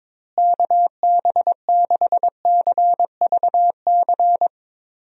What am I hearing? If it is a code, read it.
Morse code, heard as a keyed tone, about 21 wpm, 700 Hz. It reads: KB6CVC
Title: USA Call Signs - 6 Characters - Repeated 2x - 22wpm